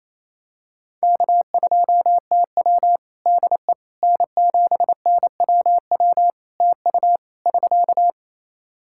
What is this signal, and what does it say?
Morse code, heard as a keyed tone, about 28 wpm, 700 Hz. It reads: K2TW DE N7NWW TU <SK>
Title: K1USN SST QSO Practice 28wpm